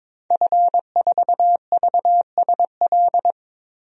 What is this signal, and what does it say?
Morse code, heard as a keyed tone, about 22 wpm, 700 Hz. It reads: F4VSL
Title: DXCC-100 Call Signs - 5 Characters - Repeated 2x - 22wpm